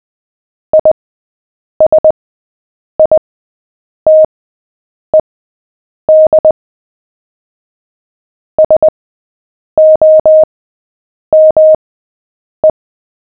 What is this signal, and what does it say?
Morse code, heard as a keyed tone, about 20 wpm, spaced as for 8 wpm, 620 Hz. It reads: ISITED SOME
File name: cw / fists / Short-Story-7-8wpm